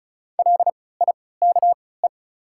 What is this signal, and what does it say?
Morse code, heard as a keyed tone, about 33 wpm, spaced as for 19 wpm, 700 Hz. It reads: LIKE
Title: 3x Character Spacing - Top 100 Words - 35wpm